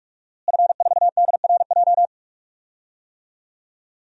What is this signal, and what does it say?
Morse code, heard as a keyed tone, about 46 wpm, 700 Hz. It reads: F4DRJ
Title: DXCC-100 Call Signs - 5 Characters - Repeated 3x - 45wpm